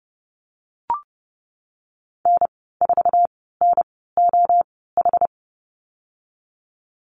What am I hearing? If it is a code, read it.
Morse code, heard as a keyed tone, about 30 wpm, spaced as for 17 wpm, 700 Hz. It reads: N4NOH
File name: Lesson 37g - Letter x - Call Signs 30wpm